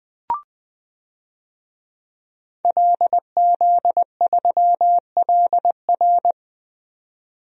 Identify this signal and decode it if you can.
Morse code, heard as a keyed tone, about 21 wpm, 700 Hz. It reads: LZ3LR
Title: DXCC-100 Call Signs - 5 Characters - 20wpm